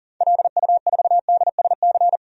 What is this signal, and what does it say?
Morse code, heard as a keyed tone, about 40 wpm, 700 Hz. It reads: LU4DSC
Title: DXCC-100 Call Signs - 6 Characters - 40wpm